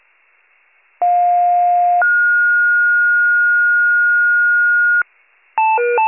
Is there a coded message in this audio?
Yes: two-tone page from 1.0 to 5.0 s, heard as a one-second tone, then a three-second tone, about 710 then 1430 Hz.